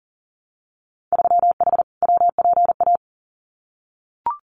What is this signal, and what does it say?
Morse code, heard as a keyed tone, about 37 wpm, 700 Hz. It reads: 3H WPA
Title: Winter Field Day Exchanges Encore 40wpm